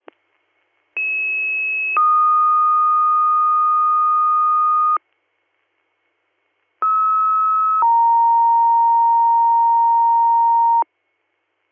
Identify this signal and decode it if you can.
Two-tone page, heard as a one-second tone, then a three-second tone, about 2600 then 1230 Hz, about 1290 then 930 Hz. It is heard from 1.0 to 10.8 s.